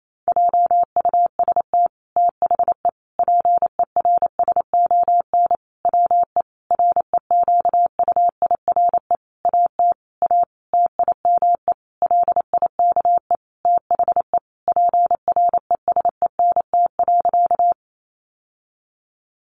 Morse code, heard as a keyed tone, about 28 wpm, 690 Hz. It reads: JUST THE PERSON WE REQUIRE AT A TIME LIKE THE PRESENT.
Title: Sentences from Top 900 Words 28wpm